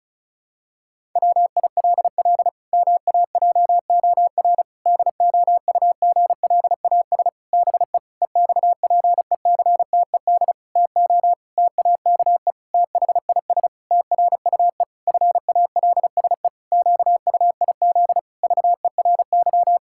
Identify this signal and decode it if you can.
Morse code, heard as a keyed tone, about 34 wpm, 700 Hz. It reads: WILL MAJOR DOUGLAS BE EXPECTED TO TAKE THIS TRUE FALSE QUIZ VERY
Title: Pangram Challenge v2 - 35wpm